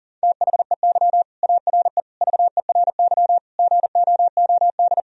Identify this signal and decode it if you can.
Morse code, heard as a keyed tone, about 40 wpm, 700 Hz. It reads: THEY ARE VERY GOOD
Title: Sentences from Top 1000 Words - Review Part 3 - 40wpm